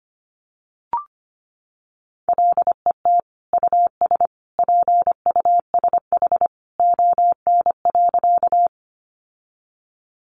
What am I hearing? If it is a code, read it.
Morse code, heard as a keyed tone, about 25 wpm, 700 Hz. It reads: LET US PUSH ON.
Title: Sentences from Top 1300 Words 25wpm